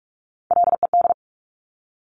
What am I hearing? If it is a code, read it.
Morse code, heard as a keyed tone, about 46 wpm, 690 Hz. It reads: LED
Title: Top Three Letter Combinations in Words 45wpm